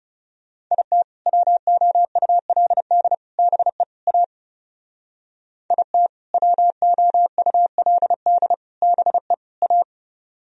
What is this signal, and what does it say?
Morse code, heard as a keyed tone, about 37 wpm, 700 Hz. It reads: IT WOULD BE A IT WOULD BE A
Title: Speed-Racing - Sets of 4 Words 30wpm